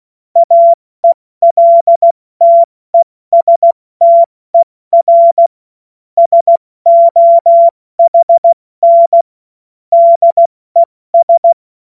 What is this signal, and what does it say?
Morse code, heard as a keyed tone, about 16 wpm, 680 Hz. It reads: AELTESTER SOHN DES
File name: Witz-29-Juli-2025-70-F80